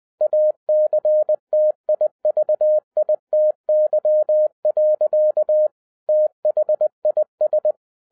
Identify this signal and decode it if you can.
Morse code, heard as a keyed tone, about 20 wpm, 610 Hz. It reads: ACTIVITY. THIS